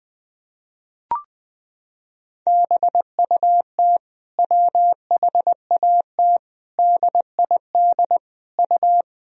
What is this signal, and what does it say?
Morse code, heard as a keyed tone, about 20 wpm, 700 Hz. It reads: BUT WHAT DID U
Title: Sentences from Top 300 Words and Abbreviations 20wpm